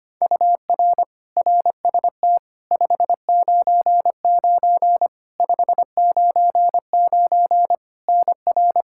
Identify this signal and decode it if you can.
Morse code, heard as a keyed tone, about 25 wpm, 700 Hz. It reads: UR RST 599 599 NR
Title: Practice QSO - Exchange 1 - Beginning - 25wpm